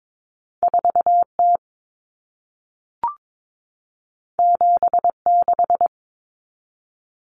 Morse code, heard as a keyed tone, about 22 wpm, 700 Hz. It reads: 4T 76
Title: Common Cut Numbers - 2-Digit 22wpm